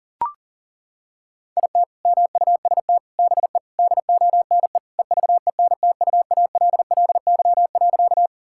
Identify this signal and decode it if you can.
Morse code, heard as a keyed tone, about 40 wpm, 700 Hz. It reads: IT MUST BE DONE EVENTUALLY.
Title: Sentences from Top 1000 Words - Review Part 6 - 40wpm